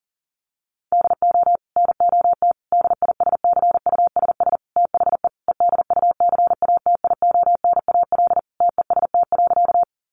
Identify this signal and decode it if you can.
Morse code, heard as a keyed tone, about 40 wpm, 700 Hz. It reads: DO NOT DISCUSS THE EDUCATIONAL TEST.
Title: Sentences from Top 1200 Words 40wpm